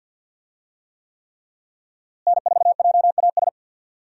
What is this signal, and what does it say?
Morse code, heard as a keyed tone, about 46 wpm, 700 Hz. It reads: N4JAS
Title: USA Call Signs - 5 Characters - Repeated 3x - 50wpm